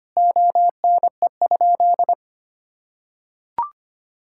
Morse code, heard as a keyed tone, about 25 wpm, 700 Hz. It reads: ONE?
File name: practice / 25wpm / Sentences with 3-Word Phrases - 25wpm